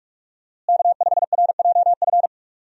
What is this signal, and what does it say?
Morse code, heard as a keyed tone, about 46 wpm, 700 Hz. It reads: K5RJF
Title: USA Call Signs - 5 Characters - Repeated 2x - 45wpm